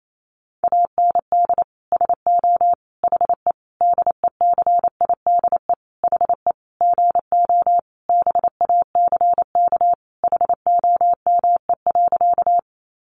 Morse code, heard as a keyed tone, about 28 wpm, 700 Hz. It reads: AND SO HE DECIDE HE GO BACK HOME.
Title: Sentences from Top 500 Words 28wpm